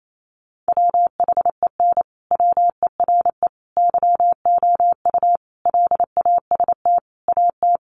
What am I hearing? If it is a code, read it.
Morse code, heard as a keyed tone, about 28 wpm, 700 Hz. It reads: WHEN WERE YOU LAST AT
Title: Sentences from Top 1300 Words 28wpm